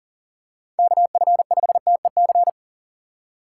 Morse code, heard as a keyed tone, about 40 wpm, 700 Hz. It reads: KF5TEC
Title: USA Call Signs - 6 Characters - 2x Character Spacing - 40wpm